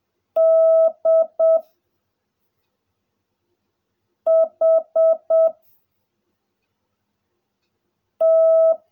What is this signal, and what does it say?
Morse code, heard as a keyed tone, about 7 wpm, 660 Hz. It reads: D H T
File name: jh8cbh / 2021cw / cw16a